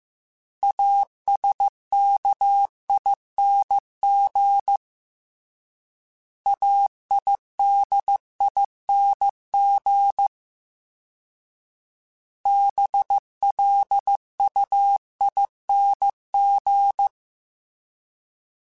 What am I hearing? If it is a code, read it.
Morse code, heard as a keyed tone, about 15 wpm, 800 Hz. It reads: ASKING AIDING BLUING
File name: Special INGs-mp3s / Special INGs-mp3s / Sngl ing @13